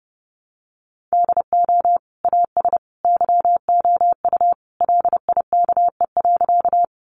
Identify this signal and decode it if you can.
Morse code, heard as a keyed tone, about 30 wpm, 700 Hz. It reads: DO AS YOU LIKE.
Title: Sentences from Top 100 Words 30wpm